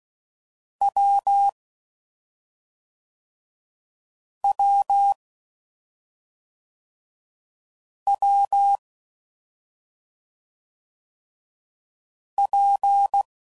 Morse code, heard as a keyed tone, about 16 wpm, 800 Hz. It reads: W W W P
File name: Combinations / Difficult Combos / WPJ WPJ1